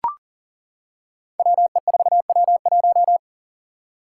Morse code, heard as a keyed tone, about 40 wpm, 700 Hz. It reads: WE4W1
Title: Five Challenging Letter-Number Combinations 40wpm